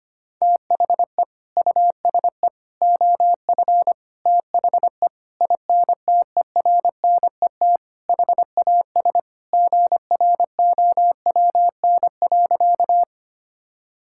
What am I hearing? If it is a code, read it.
Morse code, heard as a keyed tone, about 25 wpm, 700 Hz. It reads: THE USE OF THE INTERNET HAS GROWN.
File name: practice / 25wpm / Sentences with 4-Word Phrases - 25wpm